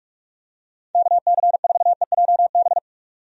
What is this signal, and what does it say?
Morse code, heard as a keyed tone, about 42 wpm, 700 Hz. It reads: KC4EJB
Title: USA Call Signs - 6 Characters - 45wpm